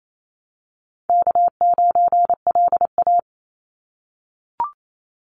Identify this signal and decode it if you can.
Morse code, heard as a keyed tone, about 28 wpm, 700 Hz. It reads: K9LA
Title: Easy Call Signs - 28wpm